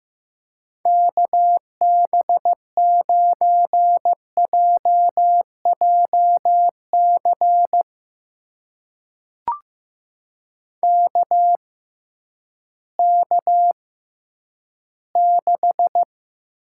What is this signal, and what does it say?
Morse code, heard as a keyed tone, about 15 wpm, spaced as for 10 wpm, 700 Hz. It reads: KB9JJC KK6
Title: USA Call Signs - 6 Characters - 6x Character Spacing - 15wpm